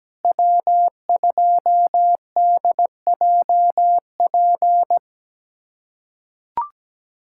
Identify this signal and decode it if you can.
Morse code, heard as a keyed tone, about 17 wpm, 700 Hz. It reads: W2DJP
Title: USA Callsign DE Callsign - 5 Characters - 17wpm